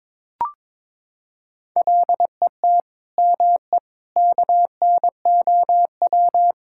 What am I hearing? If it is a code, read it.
Morse code, heard as a keyed tone, about 22 wpm, 700 Hz. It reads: LET ME KNOW
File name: Sentences with 9-Word Phrases - 22wpm